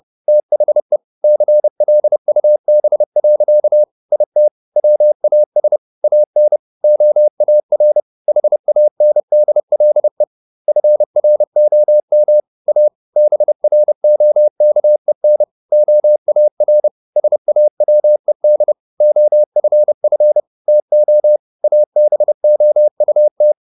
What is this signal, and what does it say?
Morse code, heard as a keyed tone, about 30 wpm, 600 Hz. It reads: THE CLUB. IT WAS AN OAR HANDLE FROM A BROKEN OAR SAWED OFF TO ABOUT